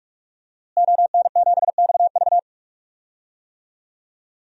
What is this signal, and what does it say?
Morse code, heard as a keyed tone, about 44 wpm, 700 Hz. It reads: ON7XV
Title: DXCC-100 Call Signs - 5 Characters - Repeated 2x - 45wpm